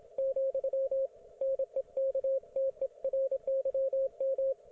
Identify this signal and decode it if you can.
Morse code, heard as a keyed tone, about 27 wpm, 540 Hz, with radio noise. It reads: , NEKTERYM